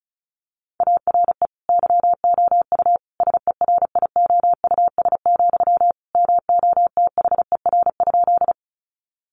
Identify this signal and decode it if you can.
Morse code, heard as a keyed tone, about 34 wpm, 700 Hz. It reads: ARE YOU SERIOUS, MOTHER?
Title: Sentences from Top 1100 Words 35wpm